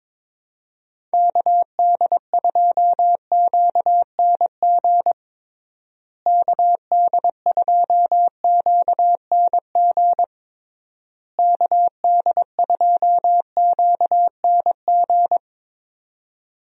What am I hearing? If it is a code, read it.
Morse code, heard as a keyed tone, about 22 wpm, 700 Hz. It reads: KD2QNG KD2QNG KD2QNG
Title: USA Call Signs - 6 Characters - Repeated 3x - 22wpm